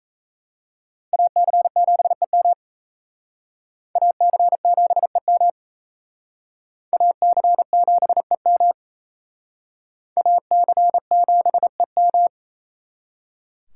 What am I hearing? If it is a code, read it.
Morse code, heard as a keyed tone, about 36 wpm, 700 Hz. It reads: AC7EM AC7EM AC7EM AC7EM